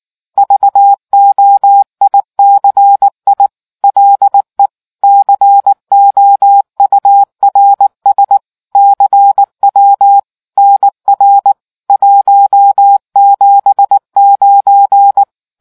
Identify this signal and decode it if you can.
Morse code, heard as a keyed tone, about 19 wpm, 800 Hz. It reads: VOICI LE COURS CW NR 179